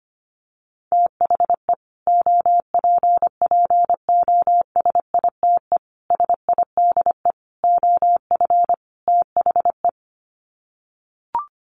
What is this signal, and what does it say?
Morse code, heard as a keyed tone, about 25 wpm, 700 Hz. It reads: THE OPPOSITE SIDE OF THE
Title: Sets of 5 Words Encore 25wpm